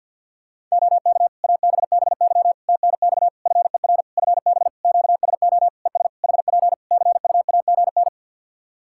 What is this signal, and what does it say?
Morse code, heard as a keyed tone, about 51 wpm, 700 Hz. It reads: OK ABBY TNX FER FB QSO ES HP CUAGN
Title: Practice QSO - Exchange 4 - Ending - 50wpm